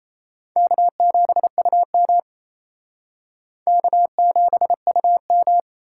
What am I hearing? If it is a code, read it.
Morse code, heard as a keyed tone, about 36 wpm, 700 Hz. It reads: K7UM K7UM